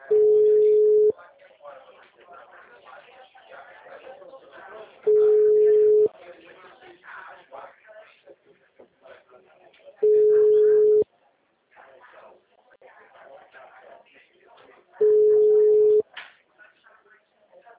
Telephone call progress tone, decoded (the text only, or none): ringback tone